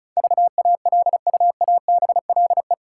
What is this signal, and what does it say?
Morse code, heard as a keyed tone, about 36 wpm, 700 Hz. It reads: VALUABLE
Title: Sentences from Top 1100 Words - New Word Drill Alt - 35wpm